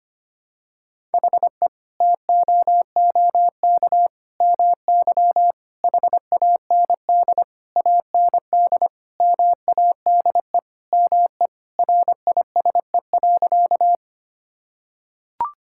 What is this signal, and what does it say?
Morse code, heard as a keyed tone, about 25 wpm, 700 Hz. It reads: HE TOOK MY HAND AND MADE ME RISE.
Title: Sentences from Top 1000 Words - Review Part 6 - 25wpm